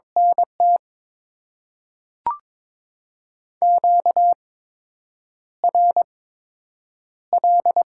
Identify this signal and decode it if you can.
Morse code, heard as a keyed tone, about 22 wpm, 700 Hz. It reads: N T Q R L
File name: Top 20 QSO Elements - 8x Character Spacing - 22wpm